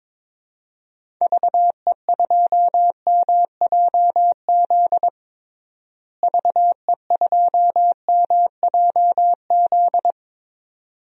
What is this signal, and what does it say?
Morse code, heard as a keyed tone, about 22 wpm, 700 Hz. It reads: VE2MJZ VE2MJZ
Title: DXCC-100 Call Signs - 6 Characters - Repeated 2x - 22wpm